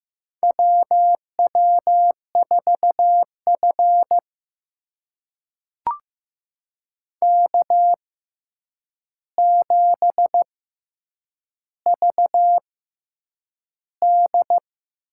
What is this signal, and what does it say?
Morse code, heard as a keyed tone, about 15 wpm, spaced as for 8 wpm, 700 Hz. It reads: WW4F K7VD
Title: USA Call Signs - 4 Characters - 6x Character Spacing - 15wpm